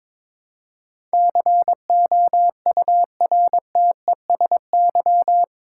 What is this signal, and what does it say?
Morse code, heard as a keyed tone, about 22 wpm, 700 Hz. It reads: COURTESY
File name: practice / 22wpm / Eight Letter Words - 22wpm